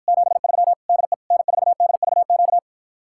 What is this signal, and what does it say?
Morse code, heard as a keyed tone, about 54 wpm, 700 Hz. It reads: 73 DE N4DVQ